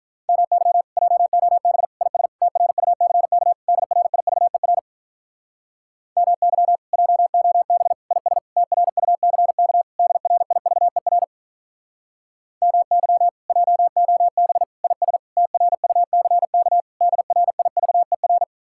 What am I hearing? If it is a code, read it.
Morse code, heard as a keyed tone, about 44 wpm, 700 Hz. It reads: MY JOB IS TRUCK DRIVER MY JOB IS TRUCK DRIVER MY JOB IS TRUCK DRIVER